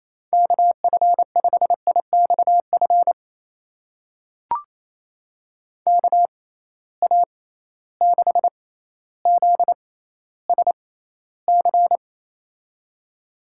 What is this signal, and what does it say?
Morse code, heard as a keyed tone, about 28 wpm, spaced as for 14 wpm, 700 Hz. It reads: KF5IXF KA6ZSC
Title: USA Call Signs - 6 Characters - 6x Character Spacing - 28wpm